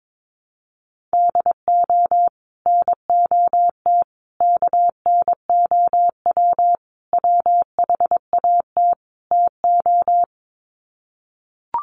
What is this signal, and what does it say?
Morse code, heard as a keyed tone, about 22 wpm, 700 Hz. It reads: DO NOT KNOW WHAT TO
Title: Sets of 5 Words 22wpm